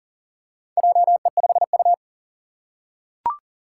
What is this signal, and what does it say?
Morse code, heard as a keyed tone, about 40 wpm, 700 Hz. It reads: JE5U
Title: Four Challenging Letter-Number Combinations 40wpm